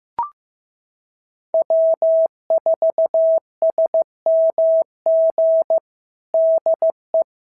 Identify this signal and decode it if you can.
Morse code, heard as a keyed tone, about 15 wpm, 650 Hz. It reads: W4SMG DE
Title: SOTA QSO Practice 15wpm